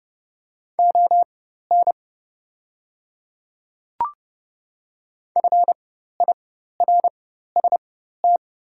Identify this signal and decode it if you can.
Morse code, heard as a keyed tone, about 30 wpm, spaced as for 14 wpm, 690 Hz. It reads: ON FIRST
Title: 4x Character Spacing - Top 100 Words - 30wpm